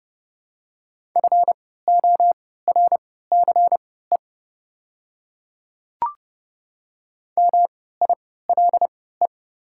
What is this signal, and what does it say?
Morse code, heard as a keyed tone, about 30 wpm, spaced as for 17 wpm, 700 Hz. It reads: FORCE MILE
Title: 3x Character Spacing - Top 500 Words - 30wpm